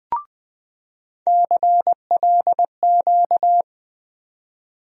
Morse code, heard as a keyed tone, about 20 wpm, 700 Hz. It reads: CLQ
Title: USA Call Sign Postfixes - 3 Letters 20wpm